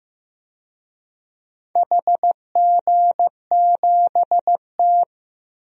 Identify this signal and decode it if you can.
Morse code, heard as a keyed tone, about 15 wpm, 700 Hz. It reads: HG7T
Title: DXCC-100 Call Signs - 4 Characters - Repeated 2x - 15wpm